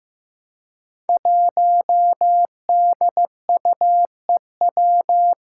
Morse code, heard as a keyed tone, about 15 wpm, 700 Hz. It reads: 1DUEW
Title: Five Challenging Letter-Number Combinations 15wpm